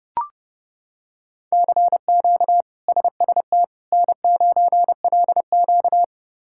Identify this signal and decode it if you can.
Morse code, heard as a keyed tone, about 30 wpm, 700 Hz. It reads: CQ SST N9LQ
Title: K1USN SST QSO Practice - 30wpm